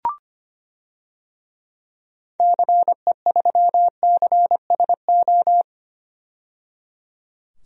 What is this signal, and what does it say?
Morse code, heard as a keyed tone, about 25 wpm, 700 Hz. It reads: CE3CSO
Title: DXCC-100 Call Signs - 6 Characters - 25wpm